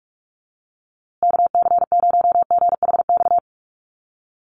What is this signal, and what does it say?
Morse code, heard as a keyed tone, about 44 wpm, 700 Hz. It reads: KC0GHX
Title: USA Call Signs Repeated 3x 45wpm